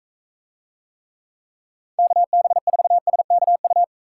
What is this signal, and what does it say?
Morse code, heard as a keyed tone, about 42 wpm, 700 Hz. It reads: KB4SKU